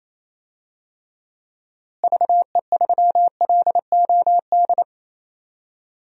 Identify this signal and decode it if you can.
Morse code, heard as a keyed tone, about 28 wpm, 700 Hz. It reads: VE3LOD
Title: DXCC-100 Call Signs - 6 Characters - Repeated 2x - 28wpm